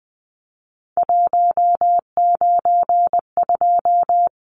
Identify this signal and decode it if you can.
Morse code, heard as a keyed tone, about 20 wpm, 700 Hz. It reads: 192
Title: Numbers - 3-Digit 20wpm